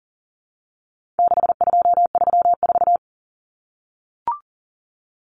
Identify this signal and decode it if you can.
Morse code, heard as a keyed tone, about 40 wpm, 700 Hz. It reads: 6234
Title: Common Cut Numbers - 4-Digit 40wpm